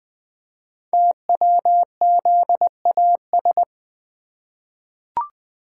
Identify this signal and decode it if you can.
Morse code, heard as a keyed tone, about 20 wpm, 700 Hz. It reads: TWZAS
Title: Code Groups - 5 Letters - 20wpm